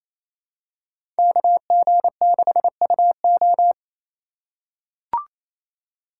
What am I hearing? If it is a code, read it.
Morse code, heard as a keyed tone, about 27 wpm, 700 Hz. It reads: KG6UO
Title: USA Call Signs - 3x Character Spacing - 28wpm